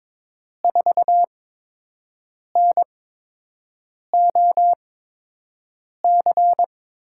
Morse code, heard as a keyed tone, about 22 wpm, spaced as for 6 wpm, 700 Hz. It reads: 4NOC
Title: USA Call Signs - 5 Characters - 8x Character Spacing - 22wpm